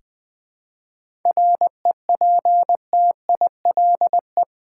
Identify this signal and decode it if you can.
Morse code, heard as a keyed tone, about 20 wpm, 700 Hz. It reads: REPTILE
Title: Difficult Words with Hints - 20wpm